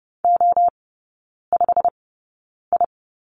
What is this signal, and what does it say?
Morse code, heard as a keyed tone, about 30 wpm, spaced as for 9 wpm, 700 Hz. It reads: O5I
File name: course / 30:10wpm / 087 Lesson 15e - Number 5 - Three Characters 30wpm